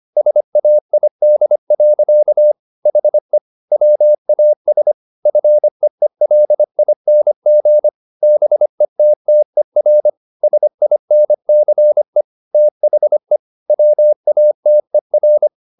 Morse code, heard as a keyed tone, about 25 wpm, 600 Hz. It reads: SAID. HE WAS FEELING BETTER SINCE THE WATER